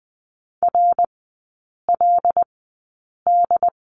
Morse code, heard as a keyed tone, about 20 wpm, spaced as for 8 wpm, 700 Hz. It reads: RLD
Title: 074 Lesson 13e - Letter L - Three Characters 20wpm